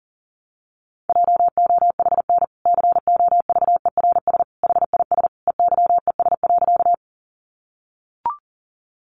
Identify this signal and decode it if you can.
Morse code, heard as a keyed tone, about 37 wpm, 700 Hz. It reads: JOHN COVERS HIS EYES.
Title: Easy Phrases 40wpm